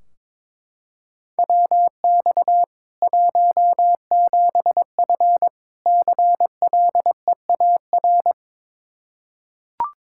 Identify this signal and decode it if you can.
Morse code, heard as a keyed tone, about 22 wpm, 700 Hz. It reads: WX 17F CLEAR
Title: Practice QSO - Exchange 2 - Info - 22wpm